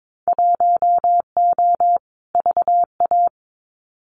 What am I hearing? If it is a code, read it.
Morse code, heard as a keyed tone, about 22 wpm, 700 Hz. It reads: 1O VA
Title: Winter Field Day Exchanges 22wpm